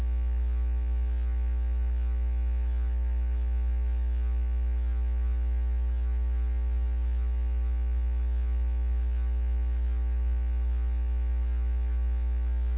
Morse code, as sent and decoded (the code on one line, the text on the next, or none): none